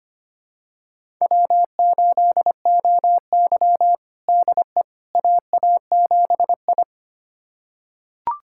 Morse code, heard as a keyed tone, about 25 wpm, 700 Hz. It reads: W8OY DE AA7I
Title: USA Callsign DE Callsign - 4 Characters - 25wpm